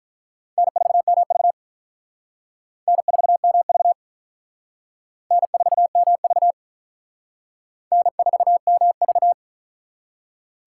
Morse code, heard as a keyed tone, about 44 wpm, 700 Hz. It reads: N4MV N4MV N4MV N4MV